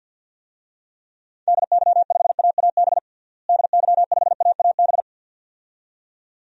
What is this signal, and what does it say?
Morse code, heard as a keyed tone, about 52 wpm, 700 Hz. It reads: DY5AAB DY5AAB
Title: DXCC-100 Call Signs - 6 Characters - Repeated 2x - 50wpm